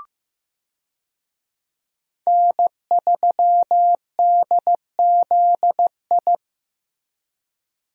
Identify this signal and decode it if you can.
Morse code, heard as a keyed tone, about 16 wpm, 700 Hz. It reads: N3DZI
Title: USA Call Signs - 5 Characters - Repeated 2x - 15wpm